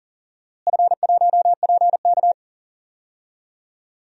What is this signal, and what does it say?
Morse code, heard as a keyed tone, about 40 wpm, 700 Hz. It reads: F1PK
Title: DXCC-100 Call Signs - 4 Characters - Repeated 3x - 40wpm